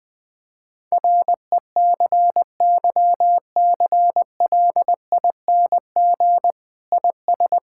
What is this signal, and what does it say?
Morse code, heard as a keyed tone, about 20 wpm, 700 Hz. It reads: RECYCLING IS
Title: Difficult Sentences with Hints - 20wpm